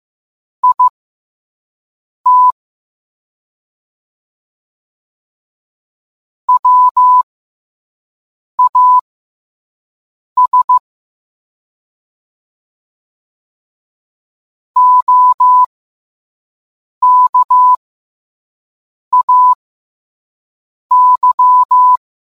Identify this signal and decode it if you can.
Morse code, heard as a keyed tone, about 15 wpm, spaced as for 5 wpm, 1000 Hz. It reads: IT WAS OKAY